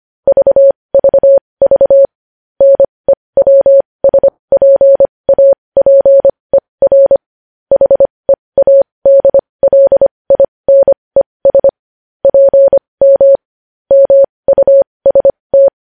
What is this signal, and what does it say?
Morse code, heard as a keyed tone, about 25 wpm, spaced as for 20 wpm, 550 Hz. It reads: VVV NEWSPAPER HEADLINES PM MUST
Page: news-headlines-in-morse-code-at-20-wpm